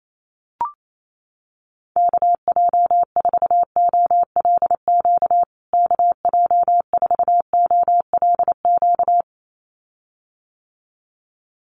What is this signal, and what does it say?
Morse code, heard as a keyed tone, about 28 wpm, 700 Hz. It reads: KJ4OLQ KJ4OLQ
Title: USA Call Signs Repeated 2x 28wpm